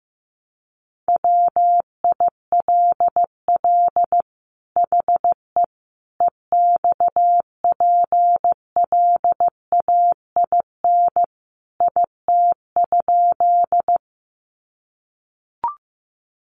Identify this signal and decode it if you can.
Morse code, heard as a keyed tone, about 15 wpm, 700 Hz. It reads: WILL HE EXPLAIN IT?
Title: Sentences from Top 1100 Words 15wpm